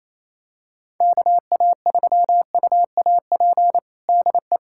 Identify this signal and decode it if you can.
Morse code, heard as a keyed tone, about 28 wpm, 700 Hz. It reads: KA3UAP DE
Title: USA Callsign DE Callsign - 6 Characters - 28wpm